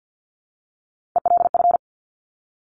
Morse code, heard as a keyed tone, about 50 wpm, 690 Hz. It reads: ELF
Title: Top Three Letter Combinations in Words 50wpm